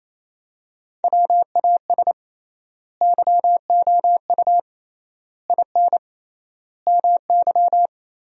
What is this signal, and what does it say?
Morse code, heard as a keyed tone, about 28 wpm, 700 Hz. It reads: WAS YOU IN MY
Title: 3x Word-Spacing - Sentences from Top 300 Words - 28wpm